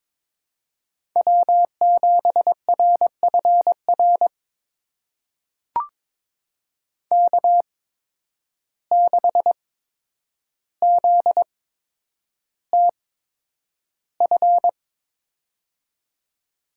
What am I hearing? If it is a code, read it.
Morse code, heard as a keyed tone, about 22 wpm, spaced as for 9 wpm, 700 Hz. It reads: W7RFR K6ZTF
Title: USA Call Signs - 5 Characters - 8x Character Spacing - 22wpm